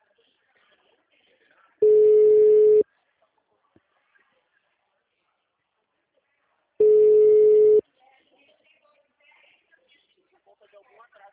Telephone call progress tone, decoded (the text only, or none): ringback tone